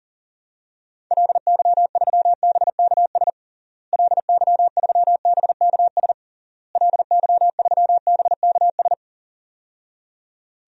Morse code, heard as a keyed tone, about 40 wpm, 700 Hz. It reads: LY3BKS LY3BKS LY3BKS
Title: DXCC-100 Call Signs - 6 Characters - Repeated 3x - 40wpm